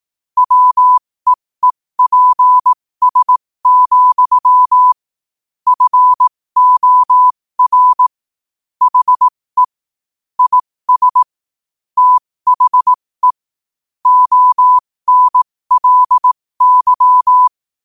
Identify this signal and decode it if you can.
Morse code, heard as a keyed tone, about 18 wpm, spaced as for 15 wpm, 1000 Hz. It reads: WEEPS, FOR HE IS THE ONLY
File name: 15wpm